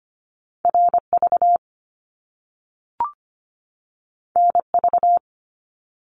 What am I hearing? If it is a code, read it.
Morse code, heard as a keyed tone, about 25 wpm, 700 Hz. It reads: RV NV
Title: Top Two Letter Combinations in Words 25wpm